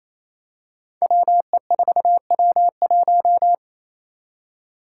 Morse code, heard as a keyed tone, about 28 wpm, 700 Hz. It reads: WE4W1
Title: Five Challenging Letter-Number Combinations 28wpm